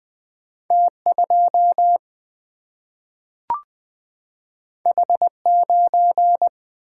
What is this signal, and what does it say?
Morse code, heard as a keyed tone, about 20 wpm, 700 Hz. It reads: T2 H9
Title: Single-Letter Single-Number 20wpm